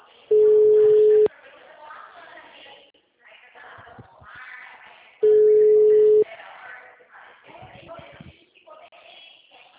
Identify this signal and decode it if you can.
Ringback tone (Europe, 425 Hz), with speech.
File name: out- -2012- - - .59164